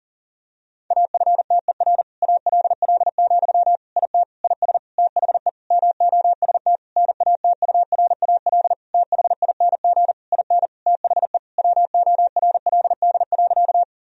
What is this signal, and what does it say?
Morse code, heard as a keyed tone, about 40 wpm, 700 Hz. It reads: AFTER ALL, IT IS THE MOST NATURAL THING IN THE WORLD.
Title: Sentences from Top 1000 Words - Review Part 4 - 40wpm